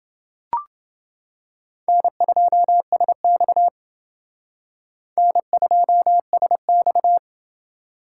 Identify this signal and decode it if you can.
Morse code, heard as a keyed tone, about 28 wpm, 700 Hz. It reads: N2SX N2SX